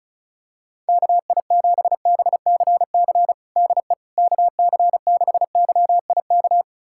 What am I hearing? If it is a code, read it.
Morse code, heard as a keyed tone, about 36 wpm, 700 Hz. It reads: KI7BCC DE KC6YIK
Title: USA Callsign DE Callsign 35wpm